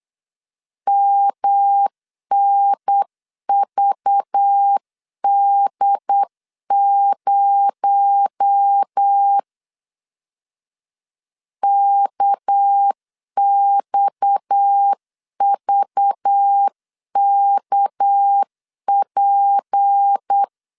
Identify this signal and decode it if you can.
Morse code, heard as a keyed tone, about 8 wpm, 800 Hz. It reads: MNVD0 KXVKP
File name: CW / DET / 1 6 1500C S C 8wpm 5219